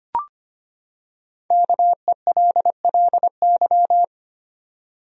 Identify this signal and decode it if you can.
Morse code, heard as a keyed tone, about 25 wpm, 700 Hz. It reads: KELLY
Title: USA Common Names - Max 5 Letters - 25wpm